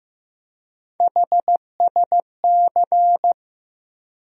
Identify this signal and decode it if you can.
Morse code, heard as a keyed tone, about 15 wpm, 700 Hz. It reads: HSC
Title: USA Call Sign Postfixes - 3 Letters 15wpm